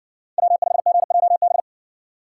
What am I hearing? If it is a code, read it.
Morse code, heard as a keyed tone, about 54 wpm, 700 Hz. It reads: W5GJB